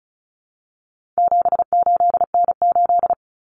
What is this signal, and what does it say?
Morse code, heard as a keyed tone, about 34 wpm, 700 Hz. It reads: 78N8
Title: Common Cut Numbers - 4-Digit 35wpm